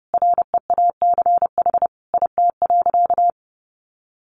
Morse code, heard as a keyed tone, about 30 wpm, 700 Hz. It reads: REACH IT.
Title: Sentences from Top 400 Words 30wpm